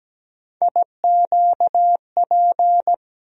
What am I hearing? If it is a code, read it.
Morse code, heard as a keyed tone, about 17 wpm, 700 Hz. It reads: IQP
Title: USA Call Sign Postfixes - 3 Letters 17wpm